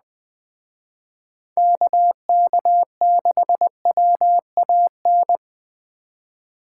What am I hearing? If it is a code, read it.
Morse code, heard as a keyed tone, about 20 wpm, 700 Hz. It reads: KK6WAN
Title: Speed Racing - USA Call Signs 17wpm